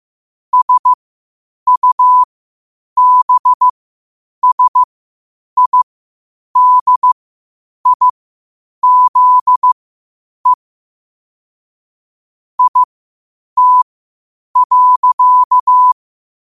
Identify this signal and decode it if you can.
Morse code, heard as a keyed tone, about 15 wpm, spaced as for 8 wpm, 1000 Hz. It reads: SUBSIDIZE IT.
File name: qotdfiles / 8wpm